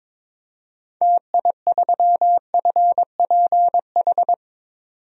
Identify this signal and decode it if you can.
Morse code, heard as a keyed tone, about 22 wpm, 700 Hz. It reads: TI3FPH
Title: DXCC-100 Call Signs - 6 Characters - 22wpm